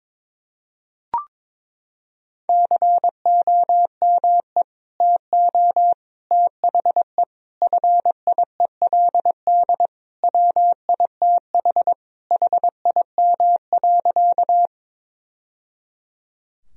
Morse code, heard as a keyed tone, about 22 wpm, 700 Hz. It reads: COME TO THE FIELD WITH HIM.
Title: Sentences from Top 1000 Words - Review Part 2 - 22wpm